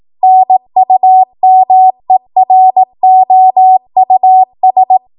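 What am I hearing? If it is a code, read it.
Morse code, heard as a keyed tone, about 18 wpm, 750 Hz. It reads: NUMEROUS
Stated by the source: 18 WPM Morse code practice from a national amateur radio association's code archive.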